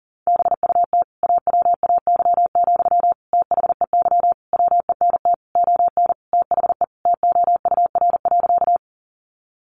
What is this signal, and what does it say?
Morse code, heard as a keyed tone, about 40 wpm, 700 Hz. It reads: BUT AWAY, THEY WENT ON THE TOUR.
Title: Sentences from Top 1400 Words 40wpm